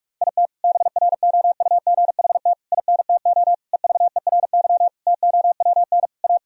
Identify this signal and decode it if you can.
Morse code, heard as a keyed tone, about 41 wpm, 700 Hz. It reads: IT BROUGHT INTO EVERY TOWN A
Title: Sentences from Top 1000 Words - Review Part 3 - 45wpm